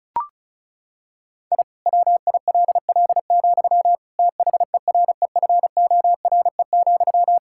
Morse code, heard as a keyed tone, about 36 wpm, 700 Hz. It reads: I WILL, THEREFORE,
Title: Sentences from Top 1000 Words - Review Part 5 - 35wpm